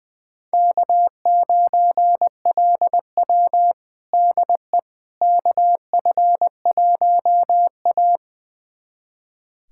Morse code, heard as a keyed tone, about 21 wpm, 700 Hz. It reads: K9LW DE KF1A
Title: USA Callsign DE Callsign - 4 Characters - 20wpm